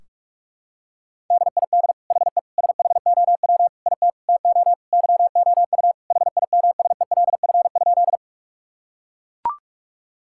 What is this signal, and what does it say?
Morse code, heard as a keyed tone, about 41 wpm, 700 Hz. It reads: DID HE SHOW IT TO YOU HIMSELF?
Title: Sentences from Top 1600 Words 45wpm